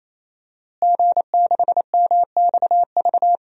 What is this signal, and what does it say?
Morse code, heard as a keyed tone, about 28 wpm, 700 Hz. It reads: G6MXV
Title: DXCC-100 Call Signs - 5 Characters - Repeated 3x - 28wpm